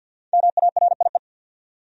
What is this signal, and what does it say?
Morse code, heard as a keyed tone, about 49 wpm, 700 Hz. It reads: MARIE
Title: USA Common Names - Max 5 Letters - 50wpm